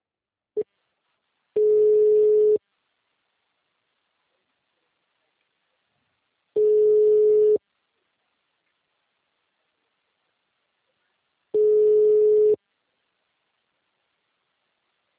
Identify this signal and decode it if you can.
Ringback tone (Europe, 425 Hz).
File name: out- -2017- - - .7073